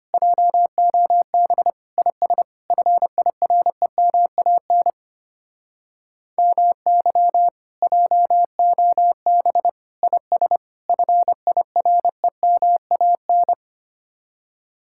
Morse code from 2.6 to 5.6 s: FIREMAN

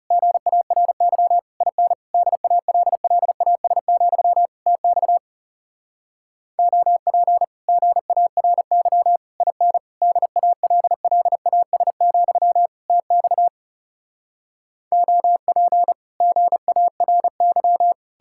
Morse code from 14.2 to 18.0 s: OP GARY

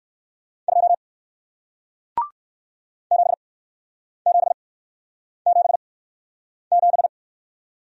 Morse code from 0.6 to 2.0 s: 2

